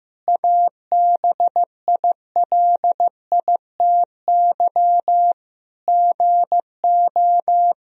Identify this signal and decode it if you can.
Morse code, heard as a keyed tone, about 15 wpm, 700 Hz. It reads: ABILITY GO
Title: Sentences from Top 1000 Words - Review Part 3 - 15wpm